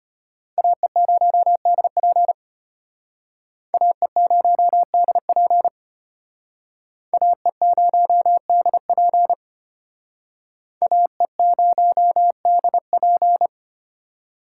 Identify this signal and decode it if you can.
Morse code, heard as a keyed tone, about 33 wpm, 700 Hz. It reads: AE0DP AE0DP AE0DP AE0DP